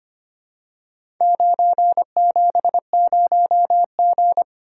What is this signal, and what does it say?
Morse code, heard as a keyed tone, about 25 wpm, 700 Hz. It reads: 970G
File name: practice / 25wpm / Four Letter-Number Combinations - 25wpm